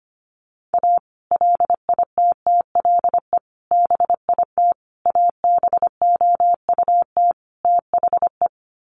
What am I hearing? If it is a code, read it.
Morse code, heard as a keyed tone, about 25 wpm, 700 Hz. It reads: A LITTLE BIT ABOUT THE
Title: Sets of 5 Words Encore2 25wpm